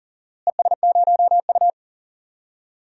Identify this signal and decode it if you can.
Morse code, heard as a keyed tone, about 40 wpm, 700 Hz. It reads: ES0U
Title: DXCC-100 Call Signs - 4 Characters - Repeated 3x - 40wpm